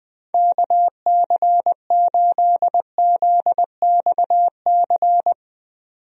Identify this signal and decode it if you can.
Morse code, heard as a keyed tone, about 20 wpm, 700 Hz. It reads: KC8ZXC
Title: USA Call Signs - 6 Characters - 20wpm